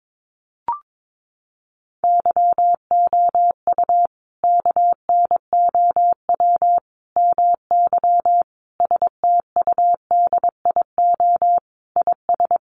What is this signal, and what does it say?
Morse code, heard as a keyed tone, about 22 wpm, 700 Hz. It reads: YOU KNOW MY STUDIO IS
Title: Sentences from Top 1200 Words 22wpm